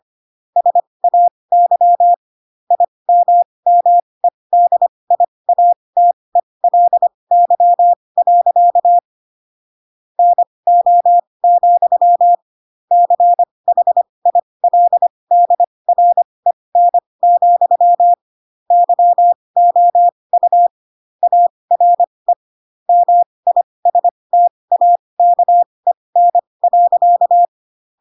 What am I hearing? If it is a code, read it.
Morse code, heard as a keyed tone, about 25 wpm, spaced as for 20 wpm, 700 Hz. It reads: SAY IMMEDIATELY. NO, CHILDREN, YOU ARE MISTAKEN.
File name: Pinocchio / Pinocchio ch1 15 25 ik1hge